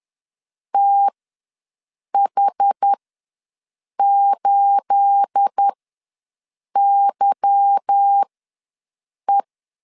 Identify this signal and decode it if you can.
Morse code, heard as a keyed tone, about 10 wpm, spaced as for 6 wpm, 800 Hz. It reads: TH8YE
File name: LEZIONI / 9 28 ws 10wpm 4124